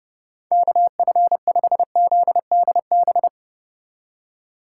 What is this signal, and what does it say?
Morse code, heard as a keyed tone, about 39 wpm, 700 Hz. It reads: KF5ZDB